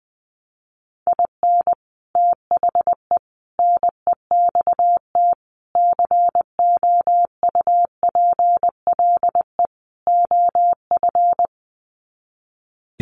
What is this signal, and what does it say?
Morse code, heard as a keyed tone, about 20 wpm, 700 Hz. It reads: IN THE NEXT COUPLE OF
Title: Sets of 5 Words Encore2 20wpm